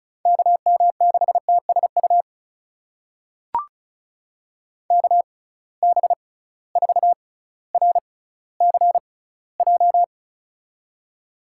Morse code, heard as a keyed tone, about 36 wpm, spaced as for 18 wpm, 700 Hz. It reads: KM6TSU KB4RCJ
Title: USA Call Signs - 6 Characters - 6x Character Spacing - 35wpm